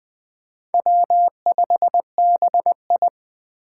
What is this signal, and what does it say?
Morse code, heard as a keyed tone, about 18 wpm, 700 Hz. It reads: W5BI
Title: Speed Racing - USA Call Signs 15wpm